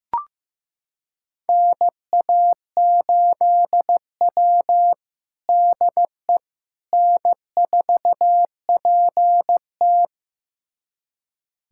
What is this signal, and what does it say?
Morse code, heard as a keyed tone, about 15 wpm, 700 Hz. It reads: NA8W DE N4PT
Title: USA Callsign DE Callsign - 4 Characters - 15wpm